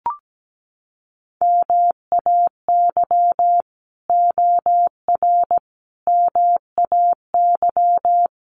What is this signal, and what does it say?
Morse code, heard as a keyed tone, about 17 wpm, 700 Hz. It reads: MAY OR MAY
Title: Sets of 5 Words 17wpm